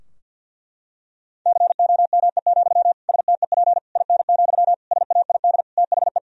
Morse code, heard as a keyed tone, about 52 wpm, 700 Hz. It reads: COME, STEP IN, SAID THE
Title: Sentences from Top 1000 Words - Review Part 2 - 50wpm